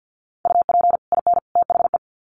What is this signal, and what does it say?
Morse code, heard as a keyed tone, about 50 wpm, 700 Hz. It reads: UP IN THE
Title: Sets of 3 Words 50wpm